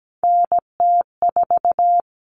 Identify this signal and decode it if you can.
Morse code, heard as a keyed tone, about 17 wpm, 700 Hz. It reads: NT4
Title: Common Cut Numbers - 3-Digit 17wpm